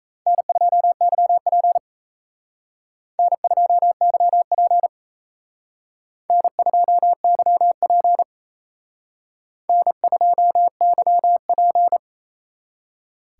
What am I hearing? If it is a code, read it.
Morse code, heard as a keyed tone, about 36 wpm, 700 Hz. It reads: N2YP N2YP N2YP N2YP